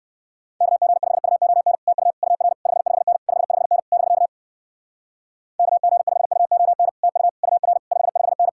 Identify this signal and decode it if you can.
Morse code, heard as a keyed tone, about 59 wpm, 700 Hz. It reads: KG5UON TU UR 55N 55N <BK> KG5UON TU UR 55N